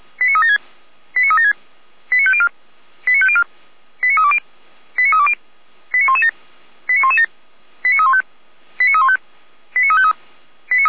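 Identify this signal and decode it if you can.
Five-tone selective call (ZVEI), heard as quick runs of five tones, tones of about 70 ms. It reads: 89376 89376 89504 89504 89320 89320 89117 89117 89315 89315 89353
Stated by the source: ZVEI five-tone selective call 89350 received by SDR.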